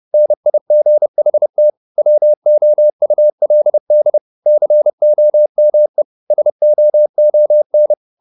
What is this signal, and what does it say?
Morse code, heard as a keyed tone, about 30 wpm, 600 Hz. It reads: NIGHT WOULD COME SOON
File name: the old man and the sea / the old man and the sea 1 30wpm